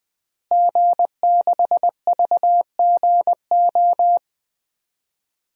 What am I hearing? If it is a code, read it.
Morse code, heard as a keyed tone, about 20 wpm, 700 Hz. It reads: G6VGO
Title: DXCC-100 Call Signs - 5 Characters - Repeated 2x - 20wpm